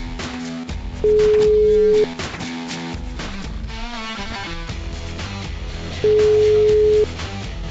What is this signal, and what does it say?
Ringback tone (Europe, 425 Hz).